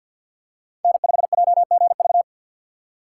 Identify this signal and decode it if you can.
Morse code, heard as a keyed tone, about 47 wpm, 700 Hz. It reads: N5JGV
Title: USA Call Signs - 6x Character Spacing - 50wpm